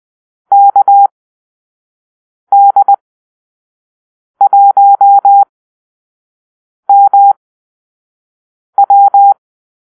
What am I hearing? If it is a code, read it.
Morse code, heard as a keyed tone, about 20 wpm, spaced as for 5 wpm, 800 Hz. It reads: KD1MW
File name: arrl / Call-Sign Drill-4